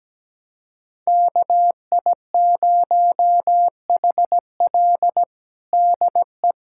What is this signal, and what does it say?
Morse code, heard as a keyed tone, about 17 wpm, 700 Hz. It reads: KI0HL DE
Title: USA Callsign DE Callsign - 5 Characters - 17wpm